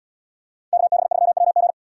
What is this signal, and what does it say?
Morse code, heard as a keyed tone, about 71 wpm, 690 Hz. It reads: KB3WG